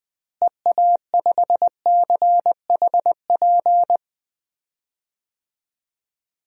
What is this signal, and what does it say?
Morse code, heard as a keyed tone, about 20 wpm, 700 Hz. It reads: EA5CHP
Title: DXCC-100 Call Signs - 6 Characters - Repeated 2x - 20wpm